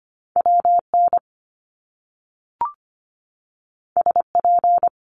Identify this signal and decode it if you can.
Morse code, heard as a keyed tone, about 25 wpm, 700 Hz. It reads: WN SP
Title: Top Two Letter Combinations in Words 25wpm